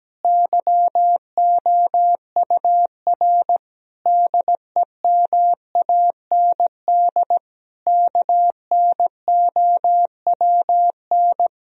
Morse code, heard as a keyed tone, about 17 wpm, 700 Hz. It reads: YOUR DEMAND KNOWN
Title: Sentences from Top 1000 Words - Review Part 4 - 17wpm